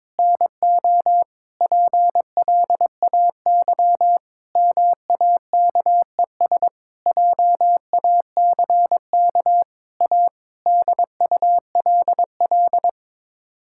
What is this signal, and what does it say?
Morse code, heard as a keyed tone, about 22 wpm, 700 Hz. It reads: NO PLAY MAKES JACK A DULL
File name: practice / 22wpm / Sets of 8 Words - 22wpm